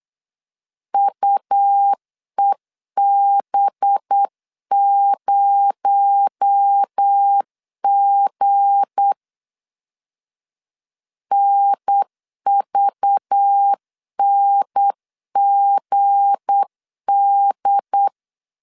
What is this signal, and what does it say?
Morse code, heard as a keyed tone, about 8 wpm, 800 Hz. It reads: UEB0G NVNGD
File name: LEZIONI / 7 GF9Z / 7 G 8wpm 3118